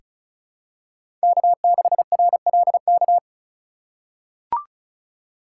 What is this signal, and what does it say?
Morse code, heard as a keyed tone, about 36 wpm, 700 Hz. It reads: K6RLK
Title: USA Call Signs - 5 Characters - 4x Character Spacing - 35wpm